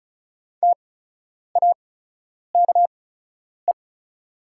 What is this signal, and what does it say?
Morse code, heard as a keyed tone, about 34 wpm, spaced as for 10 wpm, 700 Hz. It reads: TAKE
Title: 8x Character Spacing - Top 300 Words - 35wpm